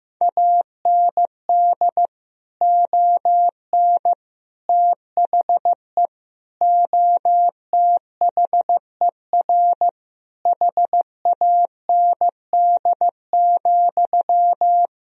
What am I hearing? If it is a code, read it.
Morse code, heard as a keyed tone, about 15 wpm, 700 Hz. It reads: AND ON THE OTHER HAND,
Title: Sentences from Top 1000 Words - Review Part 3 - 15wpm